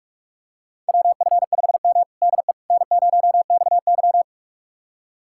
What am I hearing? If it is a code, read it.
Morse code, heard as a keyed tone, about 42 wpm, 700 Hz. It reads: WF5M DE N0XY
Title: USA Callsign DE Callsign - 4 Characters - 45wpm